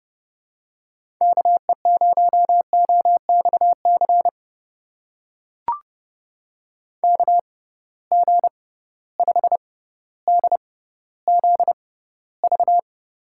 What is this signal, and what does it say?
Morse code, heard as a keyed tone, about 31 wpm, spaced as for 16 wpm, 700 Hz. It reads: KE0OXC KG5DZV
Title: USA Call Signs - 6 Characters - 6x Character Spacing - 30wpm